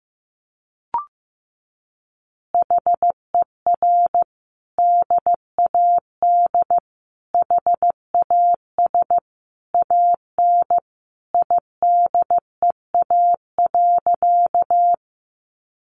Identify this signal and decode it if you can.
Morse code, heard as a keyed tone, about 15 wpm, 700 Hz. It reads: HER DAD HAS AN IDEA.
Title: Easy Phrases 15wpm